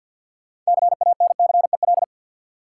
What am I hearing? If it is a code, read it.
Morse code, heard as a keyed tone, about 47 wpm, 700 Hz. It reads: CANCEL
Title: Sentences from Top 1200 Words - New Word Drill - 50wpm